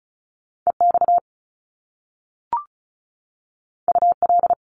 Morse code, heard as a keyed tone, about 36 wpm, 700 Hz. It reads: EX UL
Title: Top Two Letter Combinations in Words 35wpm